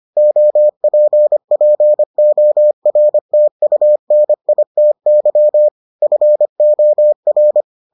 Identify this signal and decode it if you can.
Morse code, heard as a keyed tone, about 25 wpm, 600 Hz. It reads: OPPORTUNITY FOR